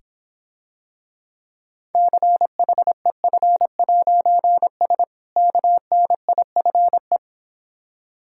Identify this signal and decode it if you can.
Morse code, heard as a keyed tone, about 26 wpm, 700 Hz. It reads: CHEF'S KNIFE